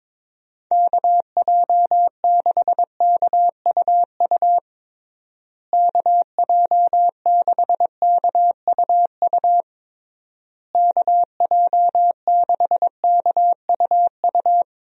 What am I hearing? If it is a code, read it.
Morse code, heard as a keyed tone, about 22 wpm, 700 Hz. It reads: KJ6KUU KJ6KUU KJ6KUU
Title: USA Call Signs - 6 Characters - Repeated 3x - 22wpm